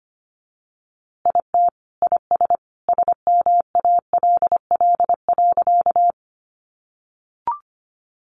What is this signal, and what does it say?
Morse code, heard as a keyed tone, about 25 wpm, 700 Hz. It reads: IT IS SMALL.
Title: Easy Phrases 25wpm